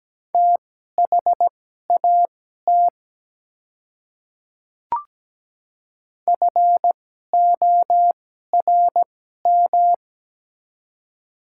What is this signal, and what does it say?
Morse code, heard as a keyed tone, about 17 wpm, spaced as for 12 wpm, 700 Hz. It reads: THAT FORM
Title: 2x Character Spacing - Top 200 Words - 17wpm